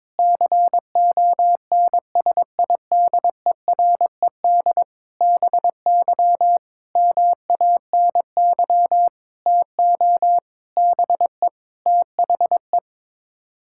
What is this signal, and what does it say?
Morse code, heard as a keyed tone, about 22 wpm, 700 Hz. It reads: CONSIDERED BY MANY TO BE THE
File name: practice / 22wpm / Sets of 7 Words - 22wpm